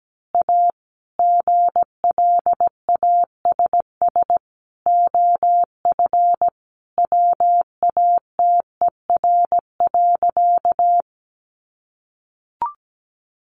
Morse code, heard as a keyed tone, about 17 wpm, 700 Hz. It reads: A GLASS OF WATER.
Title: Sentences from Top 1300 Words 17wpm